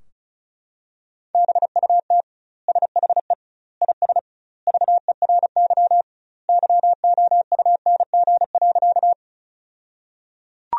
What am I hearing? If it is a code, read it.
Morse code, heard as a keyed tone, about 34 wpm, 700 Hz. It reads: BUT SHE IS VERY YOUNG.
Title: 2x Word-Spacing - Sentences from Top 400 Words - 35wpm